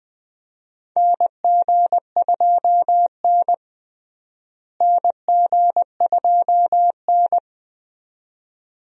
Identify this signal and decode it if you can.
Morse code, heard as a keyed tone, about 20 wpm, 700 Hz. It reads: NG2N NG2N
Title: USA Call Signs - 4 Characters - Repeated 2x - 20wpm